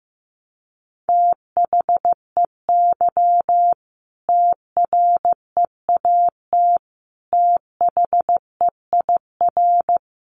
Morse code, heard as a keyed tone, about 15 wpm, 700 Hz. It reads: THEY TREAT THEIR
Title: Sentences from Top 1300 Words 15wpm